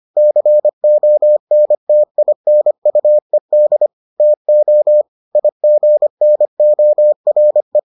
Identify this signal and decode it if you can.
Morse code, heard as a keyed tone, about 25 wpm, 600 Hz. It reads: CONTINUED TO IGNORE